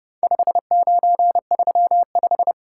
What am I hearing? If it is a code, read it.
Morse code, heard as a keyed tone, about 29 wpm, 700 Hz. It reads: 5935